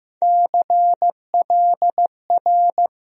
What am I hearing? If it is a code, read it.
Morse code, heard as a keyed tone, about 15 wpm, 700 Hz. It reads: CLR
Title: USA Call Sign Postfixes - 3 Letters 15wpm